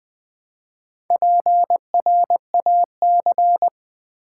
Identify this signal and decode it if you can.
Morse code, heard as a keyed tone, about 20 wpm, 700 Hz. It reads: PRAC
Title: Build A Word - 2 Letters At A Time - 8-Letter Words - 20wpm